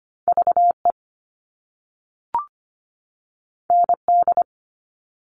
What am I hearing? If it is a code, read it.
Morse code, heard as a keyed tone, about 25 wpm, 700 Hz. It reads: VE ND
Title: All Cut Numbers - 2-Digit 25wpm